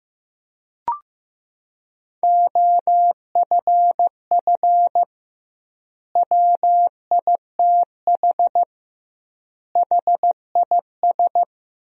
Morse code, heard as a keyed tone, about 16 wpm, 700 Hz. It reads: OFF WITH HIS
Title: 2x Word-Spacing - Sentences from Top 300 Words - 15wpm